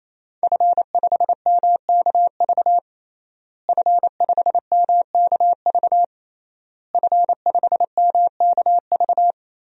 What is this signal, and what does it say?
Morse code, heard as a keyed tone, about 28 wpm, 700 Hz. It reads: F5MKV F5MKV F5MKV
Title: DXCC-100 Call Signs - 5 Characters - Repeated 3x - 28wpm